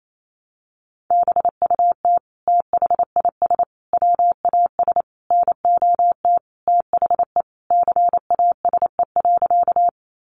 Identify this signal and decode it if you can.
Morse code, heard as a keyed tone, about 28 wpm, 700 Hz. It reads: BUT THIS WAS NOT THE CASE.
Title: Sentences from Top 700 Words 28wpm